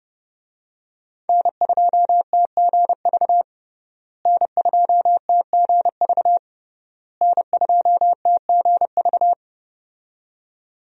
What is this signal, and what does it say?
Morse code, heard as a keyed tone, about 32 wpm, 700 Hz. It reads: N2TGV N2TGV N2TGV
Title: USA Call Signs - 5 Characters - Repeated 3x - 30wpm